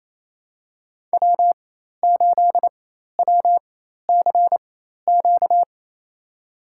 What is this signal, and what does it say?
Morse code, heard as a keyed tone, about 28 wpm, spaced as for 13 wpm, 700 Hz. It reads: W8WCQ
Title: USA Call Signs - 5 Characters - 4x Character Spacing - 28wpm